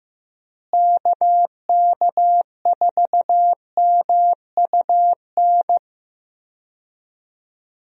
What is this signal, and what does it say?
Morse code, heard as a keyed tone, about 16 wpm, 700 Hz. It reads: KK4MUN
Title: USA Call Signs - 6 Characters - Repeated 2x - 15wpm